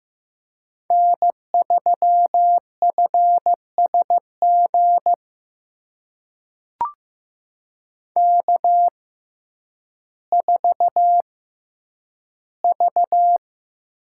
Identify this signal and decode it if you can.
Morse code, heard as a keyed tone, about 15 wpm, spaced as for 9 wpm, 700 Hz. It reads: N3FSG K4V
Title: USA Call Signs - 5 Characters - 6x Character Spacing - 15wpm